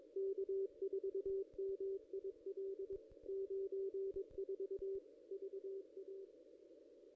Morse code, heard as a keyed tone, about 22 wpm, 390 Hz. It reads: K4MIL 94 VA
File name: contests / ss-cw-k5tr-op-n5rz -1700